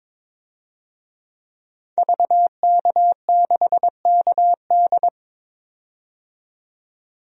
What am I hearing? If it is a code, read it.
Morse code, heard as a keyed tone, about 22 wpm, 700 Hz. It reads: VK6KD
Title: DXCC-100 Call Signs - 5 Characters - 22wpm